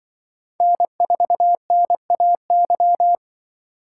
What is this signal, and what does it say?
Morse code, heard as a keyed tone, about 24 wpm, 700 Hz. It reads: N4NAY
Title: Speed Racing - USA 1x3 Call Signs 20wpm